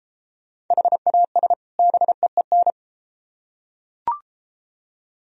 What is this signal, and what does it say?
Morse code, heard as a keyed tone, about 33 wpm, 700 Hz. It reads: HAS BEEN